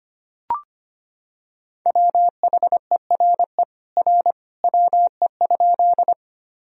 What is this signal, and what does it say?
Morse code, heard as a keyed tone, about 25 wpm, 700 Hz. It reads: WHERE R WE?
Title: Sentences from Top 200 Words and Abbreviations 25wpm